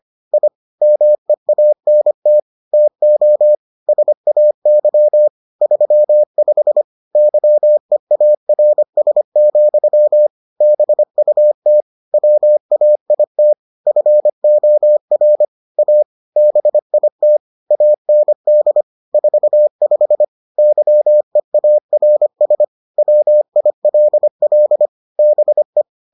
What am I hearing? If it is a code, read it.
Morse code, heard as a keyed tone, about 25 wpm, 600 Hz. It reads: I MEANT TO SAY 35 YEARS, BUT WAIT FOR A BIT AND 45 YEARS WILL BE